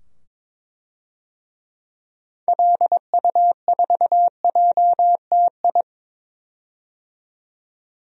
Morse code, heard as a keyed tone, about 22 wpm, 700 Hz. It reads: LU4JTI
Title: DXCC-100 Call Signs - 6 Characters - 22wpm